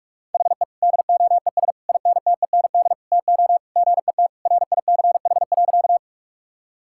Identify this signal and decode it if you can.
Morse code, heard as a keyed tone, about 46 wpm, 700 Hz. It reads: HE DOES INTEND TO GET RICH.
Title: Sentences from Top 1600 Words 45wpm